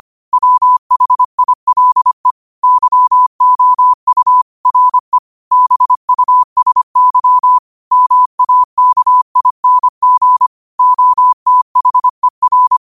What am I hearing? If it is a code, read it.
Morse code, heard as a keyed tone, about 25 wpm, 1000 Hz. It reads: WHILE YOU RE BUSY MAKING OTHER
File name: qotdfiles / 25wpm